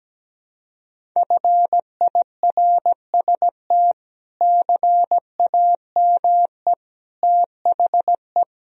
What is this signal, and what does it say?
Morse code, heard as a keyed tone, about 17 wpm, 700 Hz. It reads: FIRST CAME THE
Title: Sentences from Top 1000 Words - Review Part 1 - 17wpm